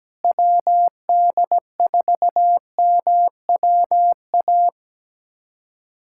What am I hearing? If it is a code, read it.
Morse code, heard as a keyed tone, about 17 wpm, 700 Hz. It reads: WD4MWA
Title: USA Callsign DE Callsign 17wpm